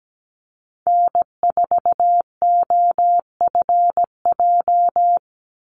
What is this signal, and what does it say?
Morse code, heard as a keyed tone, about 17 wpm, 700 Hz. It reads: N4OFJ
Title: USA Call Signs 17wpm